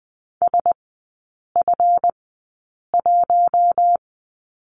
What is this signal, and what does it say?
Morse code, heard as a keyed tone, about 20 wpm, spaced as for 8 wpm, 690 Hz. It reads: SF1
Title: 198 Lesson 32e - Letter k - Three Characters 20wpm